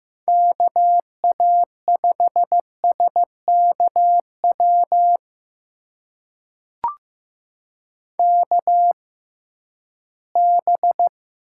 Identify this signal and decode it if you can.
Morse code, heard as a keyed tone, about 15 wpm, spaced as for 11 wpm, 700 Hz. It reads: KA5SKW KB
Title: USA Call Signs - 6 Characters - 6x Character Spacing - 15wpm